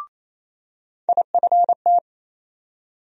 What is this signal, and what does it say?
Morse code, heard as a keyed tone, about 28 wpm, 700 Hz. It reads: IFT
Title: USA Call Sign Postfixes - 3 Letters 28wpm